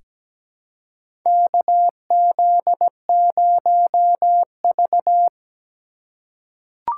Morse code, heard as a keyed tone, about 17 wpm, 700 Hz. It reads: KZ0V
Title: USA Call Signs - 4 Characters - 2x Character Spacing - 17wpm